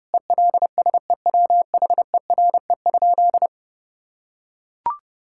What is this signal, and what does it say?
Morse code, heard as a keyed tone, about 30 wpm, 700 Hz. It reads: ELSEWHERE?
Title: Sentences from Top 1600 Words 30wpm